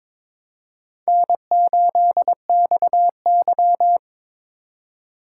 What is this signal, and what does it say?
Morse code, heard as a keyed tone, about 22 wpm, 700 Hz. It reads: N8XY
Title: USA Call Signs - 4x Character Spacing - 22wpm